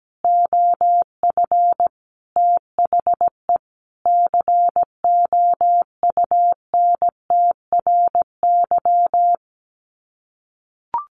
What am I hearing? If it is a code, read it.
Morse code, heard as a keyed tone, about 17 wpm, 700 Hz. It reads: OF THE COUNTRY
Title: Sets of 5 Words Encore 17wpm